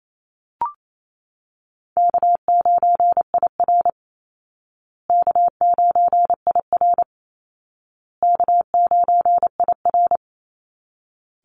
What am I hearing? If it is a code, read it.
Morse code, heard as a keyed tone, about 28 wpm, 700 Hz. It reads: K9IR K9IR K9IR
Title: Easy Call Signs Repeated 3x - 28wpm